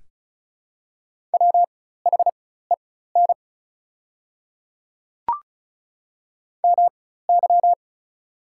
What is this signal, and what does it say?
Morse code, heard as a keyed tone, about 34 wpm, spaced as for 16 wpm, 700 Hz. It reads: WHEN MY
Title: 4x Character Spacing - Top 200 Words - 35wpm